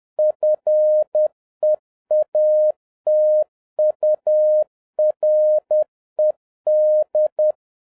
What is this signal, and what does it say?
Morse code, heard as a keyed tone, about 10 wpm, 610 Hz. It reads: FEATURED